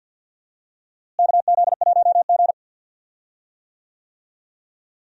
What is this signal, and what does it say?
Morse code, heard as a keyed tone, about 48 wpm, 700 Hz. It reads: KZ1G
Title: USA Call Signs - 4 Characters - Repeated 3x - 50wpm